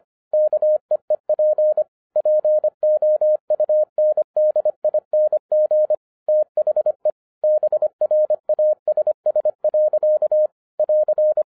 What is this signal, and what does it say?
Morse code, heard as a keyed tone, about 25 wpm, 610 Hz. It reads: KEEP POUNDING THE BRASS. <AR>